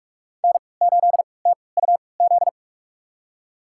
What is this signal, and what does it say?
Morse code, heard as a keyed tone, about 46 wpm, spaced as for 26 wpm, 700 Hz. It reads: N8TUZ
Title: USA Call Signs - 5 Characters - 3x Character Spacing - 45wpm